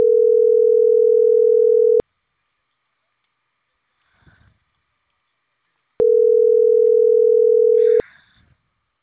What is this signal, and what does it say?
Ringback tone (North America), 440 + 480 Hz.